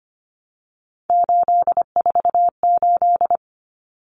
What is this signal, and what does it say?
Morse code, heard as a keyed tone, about 25 wpm, 700 Hz. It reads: 848
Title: Common Cut Numbers - 3-Digit 25wpm